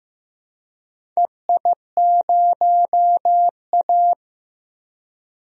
Morse code, heard as a keyed tone, about 15 wpm, 700 Hz. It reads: EI0A
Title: DXCC-100 Call Signs - 4 Characters - 15wpm